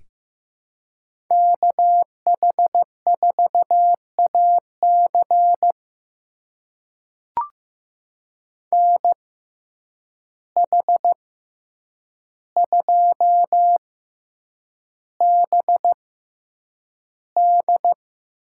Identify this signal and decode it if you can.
Morse code, heard as a keyed tone, about 15 wpm, spaced as for 8 wpm, 700 Hz. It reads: KH4AC NH2BD
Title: USA Call Signs - 5 Characters - 6x Character Spacing - 15wpm